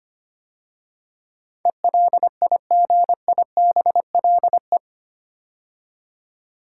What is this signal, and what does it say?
Morse code, heard as a keyed tone, about 25 wpm, 700 Hz. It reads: ELIGIBLE
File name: 8 Letter Words - First 3 Letters Given - 25wpm